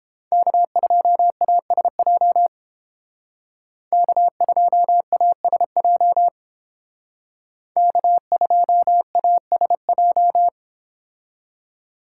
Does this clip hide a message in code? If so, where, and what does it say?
Yes: Morse code from 0.3 to 10.5 s, heard as a keyed tone, about 30 wpm, 700 Hz. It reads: K2ASJ K2ASJ K2ASJ